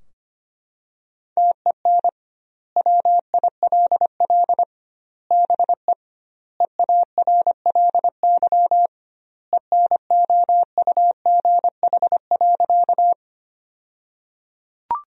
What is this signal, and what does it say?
Morse code, heard as a keyed tone, about 25 wpm, 700 Hz. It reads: TEN WILL BE EARLY ENOUGH.
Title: 2x Word-Spacing - Sentences from Top 400 Words - 25wpm